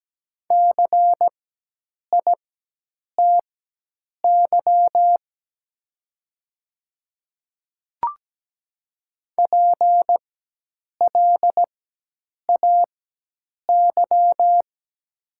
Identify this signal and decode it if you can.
Morse code, heard as a keyed tone, about 17 wpm, spaced as for 8 wpm, 700 Hz. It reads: CITY PLAY
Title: 4x Character Spacing - Top 300 Words - 17wpm